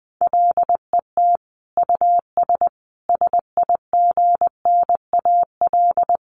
Morse code, heard as a keyed tone, about 19 wpm, 700 Hz. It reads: LET US SIGNAL
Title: Sentences from Top 1400 Words 20wpm